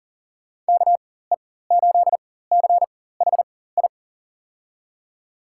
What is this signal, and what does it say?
Morse code, heard as a keyed tone, about 40 wpm, spaced as for 19 wpm, 700 Hz. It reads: KE8CHI
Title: USA Call Signs - 6 Characters - 4x Character Spacing - 40wpm